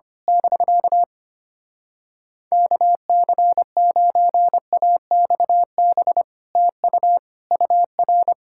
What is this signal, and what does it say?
Morse code, heard as a keyed tone, about 24 wpm, 700 Hz. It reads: <BK> KC9AXB TU UR
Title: Speed-Racing Common QSO Phrases Encore - 25wpm